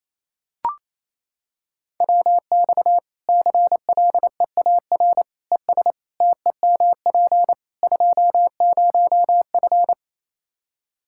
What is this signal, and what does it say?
Morse code, heard as a keyed tone, about 28 wpm, 700 Hz. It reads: WX CLEAR ES TEMP 20F
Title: K1USN SST QSO Practice 28wpm